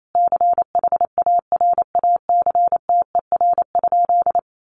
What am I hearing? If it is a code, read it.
Morse code, heard as a keyed tone, about 28 wpm, 700 Hz. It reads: CHARACTER?
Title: Sentences from Top 1200 Words 28wpm